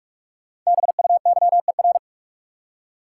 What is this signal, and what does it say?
Morse code, heard as a keyed tone, about 46 wpm, 700 Hz. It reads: BUYER
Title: Five Letter Words - 45wpm